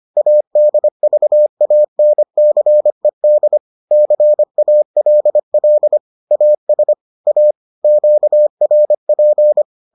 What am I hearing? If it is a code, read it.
Morse code, heard as a keyed tone, about 25 wpm, 600 Hz. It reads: ADVANCED CALL AS A QRP